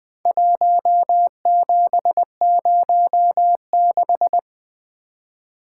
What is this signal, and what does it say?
Morse code, heard as a keyed tone, about 20 wpm, 700 Hz. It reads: 1706
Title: Four Letter-Number Combinations - 20wpm